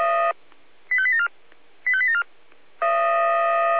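Five-tone selective call (ZVEI), heard as quick runs of five tones, tones of about 70 ms. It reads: 85874 85874
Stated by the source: ZVEI five-tone call received by SDR.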